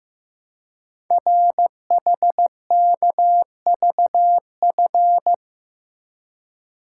Morse code, heard as a keyed tone, about 15 wpm, 700 Hz. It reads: RHKVF
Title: Code Groups - 5 Letters - 15wpm